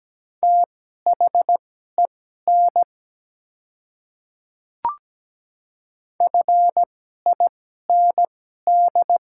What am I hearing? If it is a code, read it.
Morse code, heard as a keyed tone, about 17 wpm, spaced as for 12 wpm, 700 Hz. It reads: THEN FIND
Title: 2x Character Spacing - Top 200 Words - 17wpm